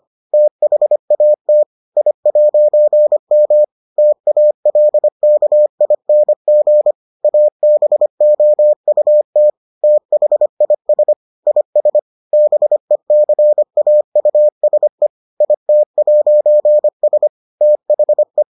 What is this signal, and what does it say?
Morse code, heard as a keyed tone, about 25 wpm, 600 Hz. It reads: THAT I'M TALKING ABOUT THIS IS BECAUSE IT'S THE